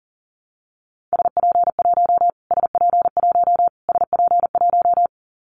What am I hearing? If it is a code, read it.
Morse code, heard as a keyed tone, about 40 wpm, 700 Hz. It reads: SP1 SP1 SP1
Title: Callsign Prefixes for Top 25 DXCC - Repeated 3x - 40wpm